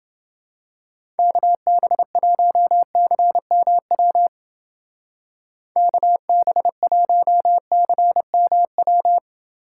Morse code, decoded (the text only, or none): KB1CMW KB1CMW